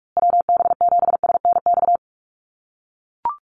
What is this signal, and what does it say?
Morse code, heard as a keyed tone, about 44 wpm, 690 Hz. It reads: WB7SNX
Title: USA Call Signs Repeated 3x 45wpm